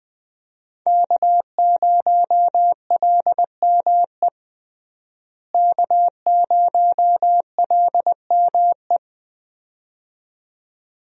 Morse code, heard as a keyed tone, about 20 wpm, 700 Hz. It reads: K0LME K0LME
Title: USA Call Signs - 5 Characters - Repeated 2x - 20wpm